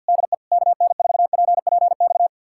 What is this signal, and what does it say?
Morse code, heard as a keyed tone, about 65 wpm, spaced as for 57 wpm, 700 Hz. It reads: DE KN4PPX
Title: Speed-Racing Common QSO Phrases - 50wpm